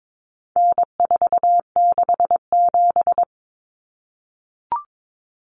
Morse code, heard as a keyed tone, about 22 wpm, 700 Hz. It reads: N467
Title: Common Cut Numbers - 4-Digit 22wpm